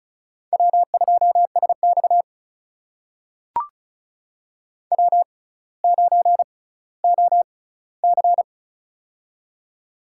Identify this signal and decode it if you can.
Morse code, heard as a keyed tone, about 36 wpm, spaced as for 18 wpm, 700 Hz. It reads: W2SX W9OC
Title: USA Call Signs - 4 Characters - 6x Character Spacing - 35wpm